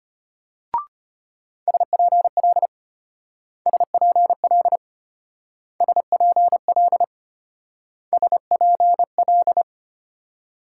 Morse code, heard as a keyed tone, about 32 wpm, 700 Hz. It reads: SPL SPL SPL SPL